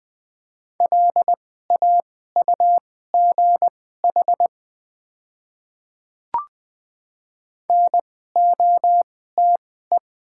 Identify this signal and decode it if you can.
Morse code, heard as a keyed tone, about 20 wpm, spaced as for 14 wpm, 690 Hz. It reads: LAUGH NOTE
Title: 2x Character Spacing - Top 500 Words - 20wpm